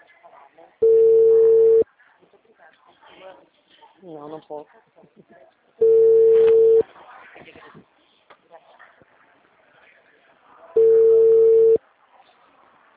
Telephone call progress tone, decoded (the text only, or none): ringback tone